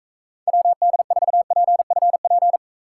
Morse code, heard as a keyed tone, about 44 wpm, 700 Hz. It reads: WD4PFP